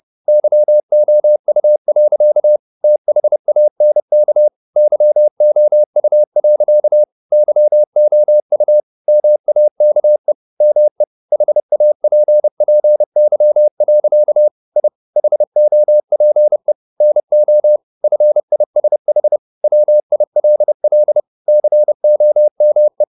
Morse code, heard as a keyed tone, about 30 wpm, 600 Hz. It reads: YOU. THANK YOU. YOU MAKE ME HAPPY. I HOPE NO FISH WILL COME
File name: the old man and the sea / the old man and the sea 1 30wpm